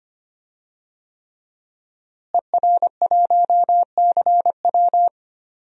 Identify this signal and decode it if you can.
Morse code, heard as a keyed tone, about 25 wpm, 700 Hz. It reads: ER1CW
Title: DXCC-100 Call Signs - 5 Characters - Repeated 3x - 25wpm